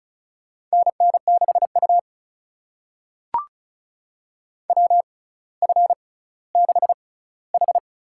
Morse code, heard as a keyed tone, about 36 wpm, spaced as for 18 wpm, 700 Hz. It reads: NN6U WF6H
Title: USA Call Signs - 4 Characters - 6x Character Spacing - 35wpm